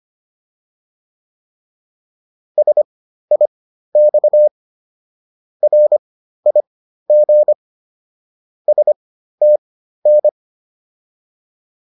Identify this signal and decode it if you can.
Morse code, heard as a keyed tone, about 25 wpm, spaced as for 13 wpm, 600 Hz. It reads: SIX RIG STN
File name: PR202 13